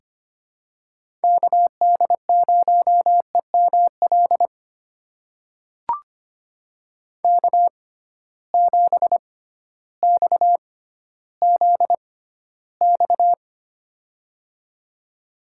Morse code, heard as a keyed tone, about 25 wpm, spaced as for 13 wpm, 700 Hz. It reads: KD0EML K7XZX
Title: USA Call Signs - 6x Character Spacing - 25wpm